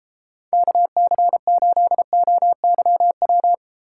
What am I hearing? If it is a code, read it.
Morse code, heard as a keyed tone, about 33 wpm, 700 Hz. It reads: KC8OYW